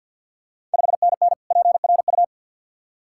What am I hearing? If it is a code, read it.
Morse code, heard as a keyed tone, about 51 wpm, 690 Hz. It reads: 5NN PRU
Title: Ontario QSO Party Exchange 50wpm